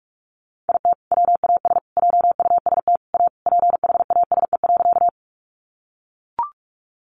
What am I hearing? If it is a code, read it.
Morse code, heard as a keyed tone, about 44 wpm, 700 Hz. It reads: IT WAS JUST A PHASE.
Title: Sentences from Top 1200 Words 45wpm